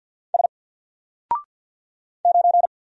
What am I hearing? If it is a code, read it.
Morse code, heard as a keyed tone, about 50 wpm, 700 Hz. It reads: S 9
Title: Single Letter-Number Alt 50wpm